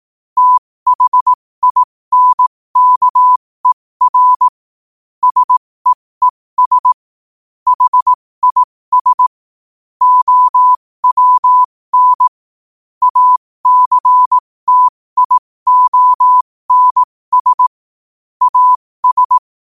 Morse code, heard as a keyed tone, about 18 wpm, spaced as for 15 wpm, 1000 Hz. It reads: THINKER SEES HIS OWN ACTIONS AS
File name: qotdfiles / 15wpm